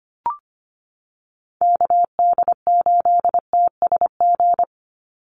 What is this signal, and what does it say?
Morse code, heard as a keyed tone, about 25 wpm, 700 Hz. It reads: KD8TSG
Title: USA Call Signs 25wpm